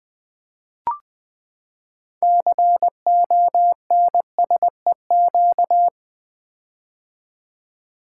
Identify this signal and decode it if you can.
Morse code, heard as a keyed tone, about 20 wpm, 700 Hz. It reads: CONSEQ
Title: Build A Word - 6 Letters At A Time - 12-Letter Words - 20wpm